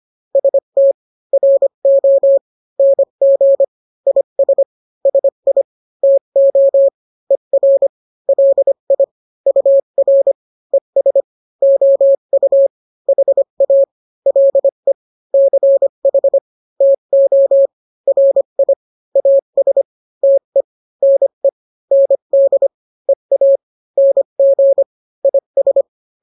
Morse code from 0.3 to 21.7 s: ST RO NG IS SI TO ER LI UR ES OU HA LE CH TO RI AS TE NE